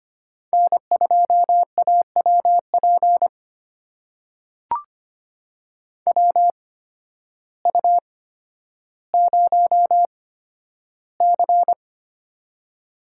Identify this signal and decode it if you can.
Morse code, heard as a keyed tone, about 25 wpm, spaced as for 12 wpm, 700 Hz. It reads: N2AWP WU0C
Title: USA Call Signs - 8x Character Spacing - 25wpm